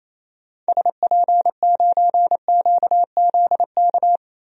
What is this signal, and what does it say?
Morse code, heard as a keyed tone, about 28 wpm, 700 Hz. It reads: SP9QZK
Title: DXCC-100 Call Signs - 6 Characters - 28wpm